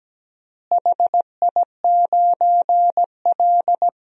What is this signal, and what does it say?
Morse code, heard as a keyed tone, about 17 wpm, 700 Hz. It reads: HI9L
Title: DXCC-100 Call Signs - 4 Characters - Repeated 2x - 17wpm